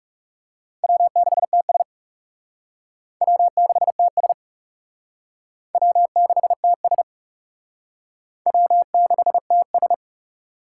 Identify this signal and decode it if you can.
Morse code, heard as a keyed tone, about 38 wpm, 700 Hz. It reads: W6TS W6TS W6TS W6TS